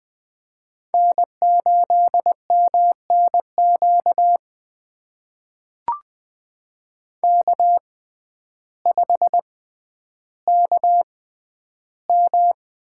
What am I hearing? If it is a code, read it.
Morse code, heard as a keyed tone, about 20 wpm, spaced as for 11 wpm, 700 Hz. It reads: N8MNQ K5KM
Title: USA Call Signs - 5 Characters - 6x Character Spacing - 20wpm